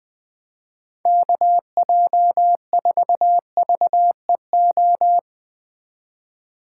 Morse code, heard as a keyed tone, about 20 wpm, 700 Hz. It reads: KJ4VEO
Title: USA Call Signs - 6 Characters - 2x Character Spacing - 20wpm